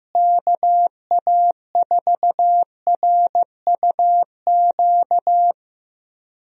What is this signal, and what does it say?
Morse code, heard as a keyed tone, about 16 wpm, 700 Hz. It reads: KA4RUQ
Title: USA Call Signs - 6 Characters - Repeated 2x - 15wpm